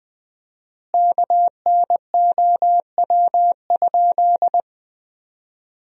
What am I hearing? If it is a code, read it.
Morse code, heard as a keyed tone, about 21 wpm, 700 Hz. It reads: KNOW?
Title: 4x Word-Spacing - Sentences from Top 100 Words - 20wpm